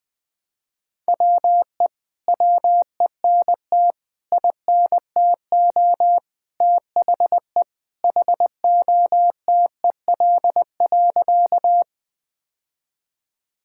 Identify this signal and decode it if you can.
Morse code, heard as a keyed tone, about 20 wpm, 700 Hz. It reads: WE WENT INTO THE HOTEL.
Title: Sentences from Top 1600 Words 20wpm